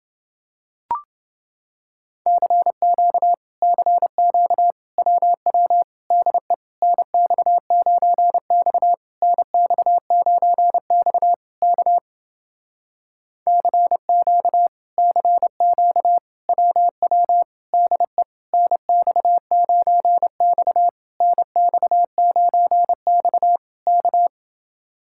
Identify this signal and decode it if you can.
Morse code, heard as a keyed tone, about 29 wpm, 700 Hz. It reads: CQ CQ WW DE NX9X NX9X K CQ CQ WW DE NX9X NX9X K